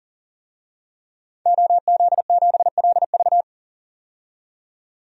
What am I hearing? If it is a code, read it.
Morse code, heard as a keyed tone, about 40 wpm, 700 Hz. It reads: OZ7LV
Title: DXCC-100 Call Signs - 5 Characters - 40wpm